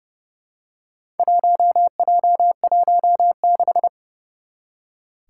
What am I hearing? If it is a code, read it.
Morse code, heard as a keyed tone, about 28 wpm, 700 Hz. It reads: 1J16
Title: Four Challenging Letter-Number Combinations 30wpm